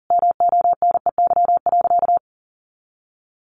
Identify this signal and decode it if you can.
Morse code, heard as a keyed tone, about 40 wpm, 700 Hz. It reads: MONEY.
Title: Easy Phrases 40wpm